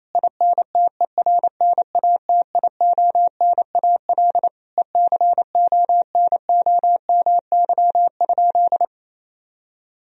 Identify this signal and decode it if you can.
Morse code, heard as a keyed tone, about 28 wpm, 700 Hz. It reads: INTERNATIONAL ECONOMY?
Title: Sentences from Top 1000 Words - Review Part 4 - 28wpm